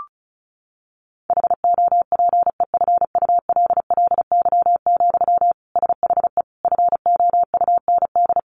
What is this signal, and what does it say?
Morse code, heard as a keyed tone, about 34 wpm, 700 Hz. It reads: HOPEFULLY, SHE FOUND
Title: Sentences from Top 1500 Words 35wpm